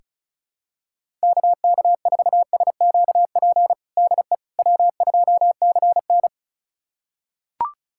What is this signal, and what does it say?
Morse code, heard as a keyed tone, about 36 wpm, 700 Hz. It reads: KK4SQP DE W2CN
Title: USA Callsign DE Callsign 35wpm